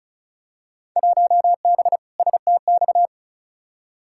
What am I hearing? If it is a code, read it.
Morse code, heard as a keyed tone, about 35 wpm, 700 Hz. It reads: 1B STX
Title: Field Day Exchanges Encore - 35wpm